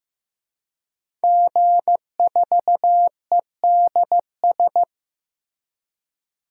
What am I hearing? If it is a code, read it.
Morse code, heard as a keyed tone, about 15 wpm, 700 Hz. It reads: G4EDS
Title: DXCC-100 Call Signs - 5 Characters - Repeated 2x - 15wpm